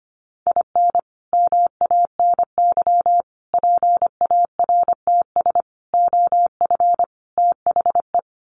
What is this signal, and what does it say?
Morse code, heard as a keyed tone, about 25 wpm, 700 Hz. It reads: IN MANY PARTS OF THE
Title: Sets of 5 Words Encore 25wpm